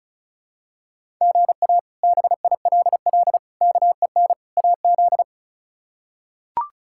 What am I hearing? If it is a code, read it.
Morse code, heard as a keyed tone, about 34 wpm, 700 Hz. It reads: GA BILL KEN AZ
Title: K1USN SST QSO Practice - 35wpm